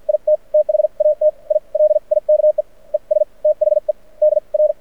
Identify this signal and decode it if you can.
Morse code, heard as a keyed tone, about 53 wpm, spaced as for 42 wpm, 610 Hz. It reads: IT THAT I LIKE ES THE DR
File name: archived / CRSnet